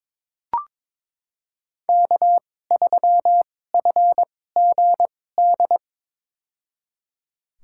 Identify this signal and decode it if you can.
Morse code, heard as a keyed tone, about 22 wpm, spaced as for 16 wpm, 700 Hz. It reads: K3FGD
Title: USA Call Signs - 2x Character Spacing - 22wpm